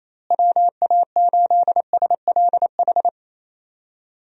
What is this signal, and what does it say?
Morse code, heard as a keyed tone, about 28 wpm, 700 Hz. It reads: WA8SLH
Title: USA Call Signs - 6 Characters - Repeated 3x - 28wpm